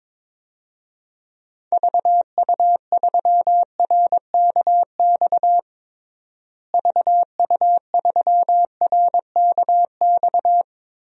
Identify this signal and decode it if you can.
Morse code, heard as a keyed tone, about 22 wpm, 700 Hz. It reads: VU3RKX VU3RKX
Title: DXCC-100 Call Signs - 6 Characters - Repeated 3x - 22wpm